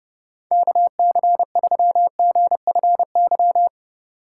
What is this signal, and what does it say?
Morse code, heard as a keyed tone, about 29 wpm, 700 Hz. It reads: KC3GFY